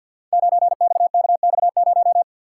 Morse code, heard as a keyed tone, about 51 wpm, 700 Hz. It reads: 9XKX0
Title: Code Groups - 5 Letter:Numbers - 50wpm